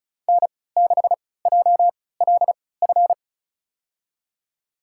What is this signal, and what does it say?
Morse code, heard as a keyed tone, about 36 wpm, spaced as for 20 wpm, 700 Hz. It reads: N6JLF
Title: USA Call Signs - 3x Character Spacing - 35wpm